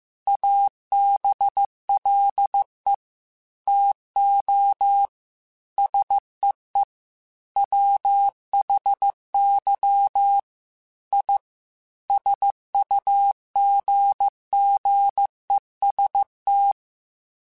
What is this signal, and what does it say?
Morse code, heard as a keyed tone, about 15 wpm, 800 Hz. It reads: ABLE TO SEE WHY I SUGGEST
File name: Technical Subjects / Technical Subjects-mp3s / OR Gate @13